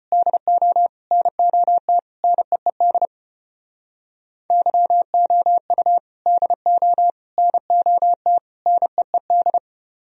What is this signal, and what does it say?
Morse code, heard as a keyed tone, about 34 wpm, 700 Hz. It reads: DO NOT NEED YOU DO NOT NEED